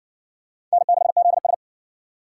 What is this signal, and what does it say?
Morse code, heard as a keyed tone, about 56 wpm, 700 Hz. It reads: N6ZS